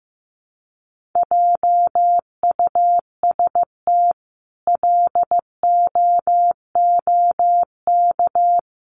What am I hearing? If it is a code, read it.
Morse code, heard as a keyed tone, about 15 wpm, 700 Hz. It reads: JUST LOOK
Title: Sentences from Top 1300 Words 15wpm